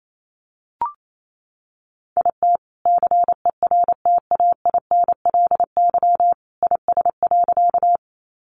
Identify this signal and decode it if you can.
Morse code, heard as a keyed tone, about 28 wpm, 700 Hz. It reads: IT CERTAINLY IS.
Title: Sentences from Top 900 Words 28wpm